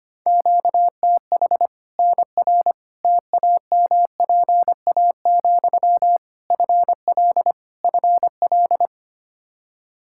Morse code, heard as a keyed tone, about 25 wpm, 700 Hz. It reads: QTH NR TAMPA, FL FL
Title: K1USN SST QSO Practice 25wpm